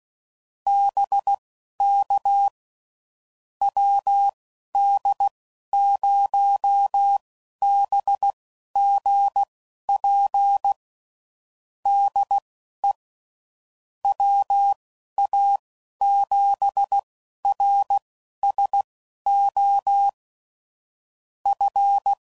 Morse code, heard as a keyed tone, about 16 wpm, spaced as for 11 wpm, 800 Hz. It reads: BK WD0BGP DE WA7RSO F
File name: CW Code Groups @15WPM / QSOs / WD0BGP Spaced